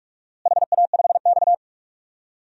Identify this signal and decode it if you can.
Morse code, heard as a keyed tone, about 46 wpm, 700 Hz. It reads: HA5X
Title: DXCC-100 Call Signs - 4 Characters - Repeated 2x - 45wpm